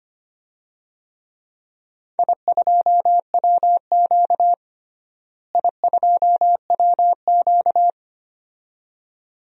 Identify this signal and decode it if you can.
Morse code, heard as a keyed tone, about 25 wpm, 700 Hz. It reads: I2WQ I2WQ
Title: DXCC-100 Call Signs - 4 Characters - Repeated 2x - 25wpm